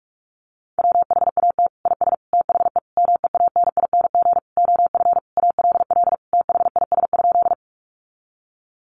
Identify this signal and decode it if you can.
Morse code, heard as a keyed tone, about 44 wpm, 700 Hz. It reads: WHAT IS THE MEANING OF ALL THIS?
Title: Sentences from Top 900 Words 45wpm